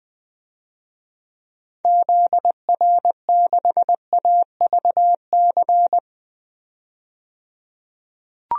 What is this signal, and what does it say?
Morse code, heard as a keyed tone, about 20 wpm, 700 Hz. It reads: ZR6AVC
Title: DXCC-100 Call Signs - 6 Characters - 20wpm